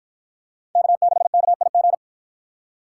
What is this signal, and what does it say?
Morse code, heard as a keyed tone, about 53 wpm, 700 Hz. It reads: K6KIG